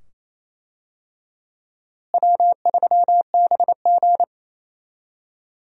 Morse code, heard as a keyed tone, about 28 wpm, 700 Hz. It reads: W3BG
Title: USA Call Signs - 4 Characters - Repeated 2x - 28wpm